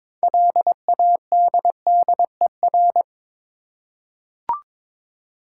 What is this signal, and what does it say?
Morse code, heard as a keyed tone, about 21 wpm, 700 Hz. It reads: LADDER
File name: Difficult Words with Hints - 22wpm